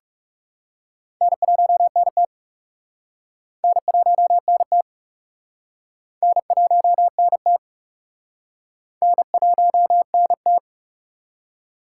Morse code, heard as a keyed tone, about 36 wpm, 700 Hz. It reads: N1NT N1NT N1NT N1NT
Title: Speed Racing - USA 1x2 Call Signs 30wpm